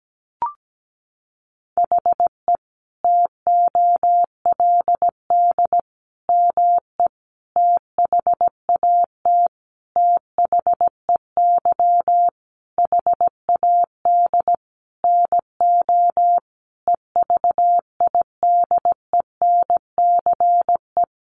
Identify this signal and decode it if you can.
Morse code, heard as a keyed tone, about 17 wpm, 700 Hz. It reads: HE TOLD ME THAT THEY HAD NO EVIDENCE
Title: Sentences from Top 1300 Words 17wpm